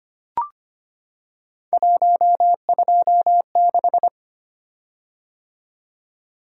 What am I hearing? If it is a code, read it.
Morse code, heard as a keyed tone, about 25 wpm, 700 Hz. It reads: 126
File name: Three Letter-Number Combinations - 25wpm